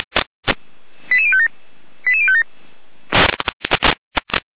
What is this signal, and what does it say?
Five-tone selective call (ZVEI), heard as quick runs of five tones, tones of about 70 ms. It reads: 88057 88057